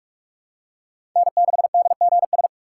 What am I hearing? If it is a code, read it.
Morse code, heard as a keyed tone, about 41 wpm, 700 Hz. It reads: N6DGS
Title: USA Call Signs - 5 Characters - 2x Character Spacing - 45wpm